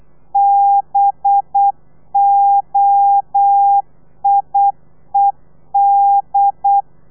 Morse code, heard as a keyed tone, about 8 wpm, 800 Hz. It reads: BOIED
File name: telegrafia / i2tfj / Lez4 35